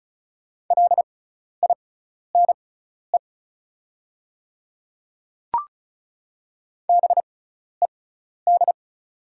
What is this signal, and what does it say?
Morse code, heard as a keyed tone, about 34 wpm, spaced as for 12 wpm, 690 Hz. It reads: LINE BED
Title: 6x Character Spacing - Top 500 Words - 35wpm